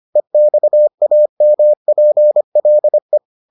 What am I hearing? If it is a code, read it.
Morse code, heard as a keyed tone, about 25 wpm, 600 Hz. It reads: EXAMPLE